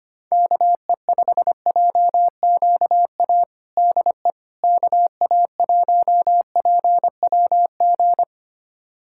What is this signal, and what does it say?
Morse code, heard as a keyed tone, about 25 wpm, 700 Hz. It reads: KE5JQA DE KA1PWG
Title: USA Callsign DE Callsign 25wpm